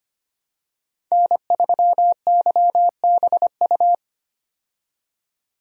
Morse code, heard as a keyed tone, about 25 wpm, 700 Hz. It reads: N3YBU
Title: USA Call Signs - 5 Characters - 25wpm